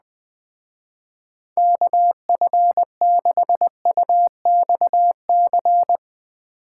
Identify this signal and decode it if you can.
Morse code, heard as a keyed tone, about 20 wpm, 700 Hz. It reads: KF6UXC
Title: Speed Racing - USA Call Signs 17wpm